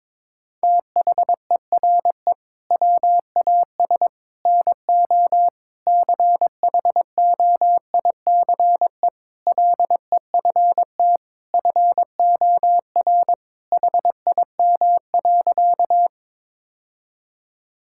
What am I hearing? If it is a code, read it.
Morse code, heard as a keyed tone, about 21 wpm, 700 Hz. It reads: THERE WAS NO CHOICE LEFT FOR HIM.
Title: Sentences from Top 1000 Words - Review Part 6 - 22wpm